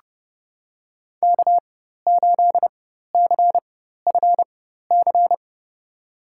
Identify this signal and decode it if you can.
Morse code, heard as a keyed tone, about 31 wpm, spaced as for 14 wpm, 700 Hz. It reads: K8CFC
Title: USA Call Signs - 5 Characters - 4x Character Spacing - 30wpm